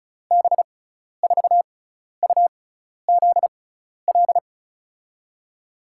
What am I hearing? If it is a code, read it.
Morse code, heard as a keyed tone, about 34 wpm, spaced as for 12 wpm, 700 Hz. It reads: B4UZL
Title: USA Call Signs - 6 Characters - 6x Character Spacing - 35wpm